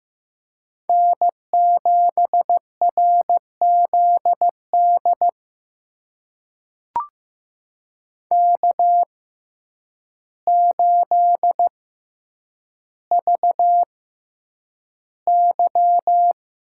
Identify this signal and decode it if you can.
Morse code, heard as a keyed tone, about 15 wpm, spaced as for 8 wpm, 700 Hz. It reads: N7RZD K8VY
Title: USA Call Signs - 6x Character Spacing - 15wpm